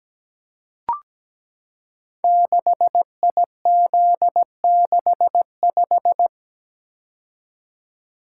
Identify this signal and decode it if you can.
Morse code, heard as a keyed tone, about 17 wpm, 700 Hz. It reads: 6IZ65
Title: Code Groups - 5 Letter:Numbers - 17wpm